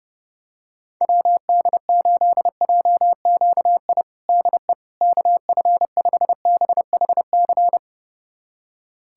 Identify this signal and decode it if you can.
Morse code, heard as a keyed tone, about 30 wpm, 700 Hz. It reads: WD8JQI DE KF5BHC
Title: USA Callsign DE Callsign 30wpm